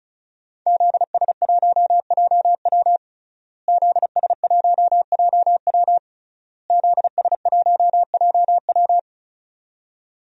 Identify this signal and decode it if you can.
Morse code, heard as a keyed tone, about 36 wpm, 700 Hz. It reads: ZS1JW ZS1JW ZS1JW
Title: DXCC-100 Call Signs - 5 Characters - Repeated 3x - 35wpm